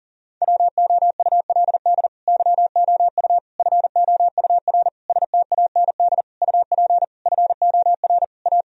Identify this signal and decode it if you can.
Morse code, heard as a keyed tone, about 40 wpm, 700 Hz. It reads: WOULD YOU FOUR STAND UP FOR A
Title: Sentences from Top 1000 Words - Review Part 1 - 40wpm